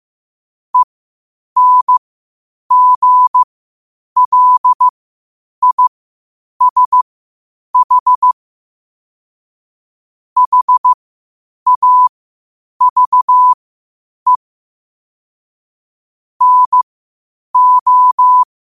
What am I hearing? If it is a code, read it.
Morse code, heard as a keyed tone, about 15 wpm, spaced as for 8 wpm, 1000 Hz. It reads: ENGLISH HAVE NO
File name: qotdfiles / 8wpm